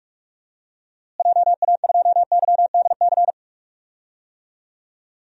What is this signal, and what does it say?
Morse code, heard as a keyed tone, about 46 wpm, 700 Hz. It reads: JA2YDC
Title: DXCC-100 Call Signs - 6 Characters - Repeated 3x - 45wpm